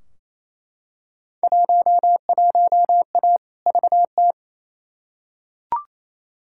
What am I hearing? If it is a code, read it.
Morse code, heard as a keyed tone, about 28 wpm, 700 Hz. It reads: 11A VT
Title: Field Day Exchanges Encore - 28wpm